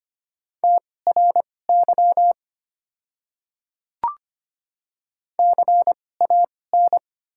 Morse code, heard as a keyed tone, about 25 wpm, spaced as for 18 wpm, 700 Hz. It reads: TRY CAN
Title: 2x Character Spacing - Top 300 Words - 25wpm